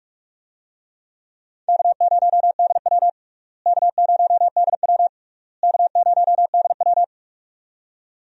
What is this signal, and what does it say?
Morse code, heard as a keyed tone, about 41 wpm, 700 Hz. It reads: K0DW K0DW K0DW
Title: USA Call Signs - 4 Characters - Repeated 3x - 45wpm